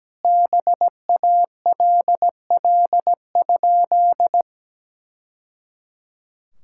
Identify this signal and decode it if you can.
Morse code, heard as a keyed tone, about 17 wpm, 700 Hz. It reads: BALL?
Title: Sentences from Top 1000 Words - Review Part 6 - 17wpm